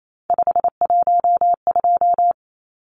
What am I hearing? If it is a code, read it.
Morse code, heard as a keyed tone, about 28 wpm, 700 Hz. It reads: 512
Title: Numbers - 3-Digit 28wpm